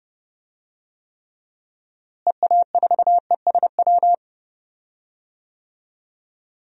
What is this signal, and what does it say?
Morse code, heard as a keyed tone, about 30 wpm, 700 Hz. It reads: EA4ESW
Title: DXCC-100 Call Signs - 6 Characters - Repeated 2x - 30wpm